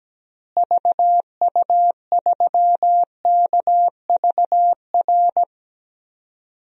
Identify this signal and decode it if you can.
Morse code, heard as a keyed tone, about 17 wpm, 700 Hz. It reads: VU3KVR
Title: DXCC-100 Call Signs - 6 Characters - Repeated 3x - 17wpm